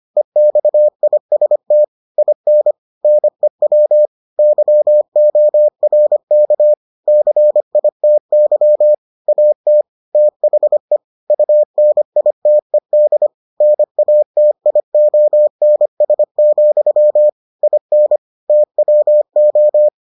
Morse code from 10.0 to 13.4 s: THE UNITED

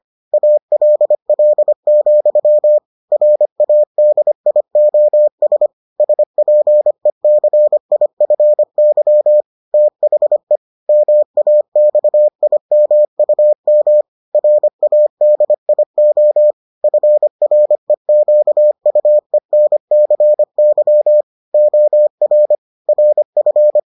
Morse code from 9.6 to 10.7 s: THE

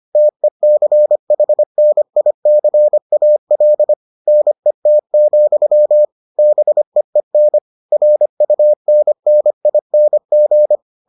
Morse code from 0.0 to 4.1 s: TECHNICAL

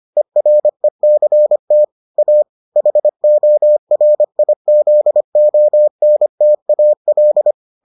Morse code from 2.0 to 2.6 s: A

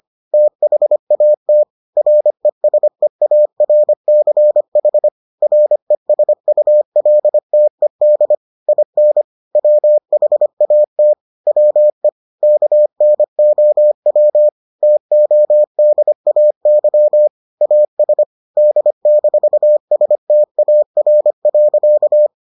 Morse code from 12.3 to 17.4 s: KNOW TODAY

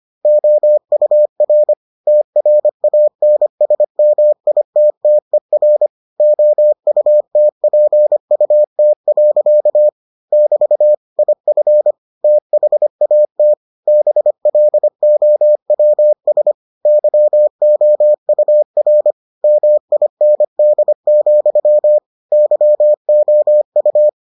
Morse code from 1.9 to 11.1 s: TRANSMITTER OUTPUT. <BT>

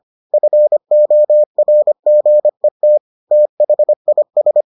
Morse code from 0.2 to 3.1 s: FORGET

